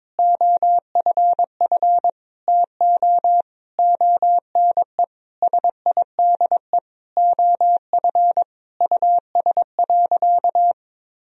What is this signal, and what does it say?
Morse code, heard as a keyed tone, about 22 wpm, 700 Hz. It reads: OFF TO ONE SIDE OF US.
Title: Sentences from Top 1000 Words - Review Part 3 - 22wpm